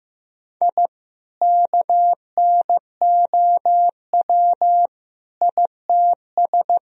Morse code, heard as a keyed tone, about 15 wpm, 700 Hz. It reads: I KNOW ITS
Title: Sentences from Top 1000 Words - Review Part 2 - 15wpm